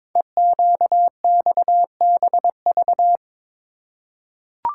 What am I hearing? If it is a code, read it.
Morse code, heard as a keyed tone, about 22 wpm, 700 Hz. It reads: EQXBV
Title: Code Groups - 5 Letter:Numbers - 22wpm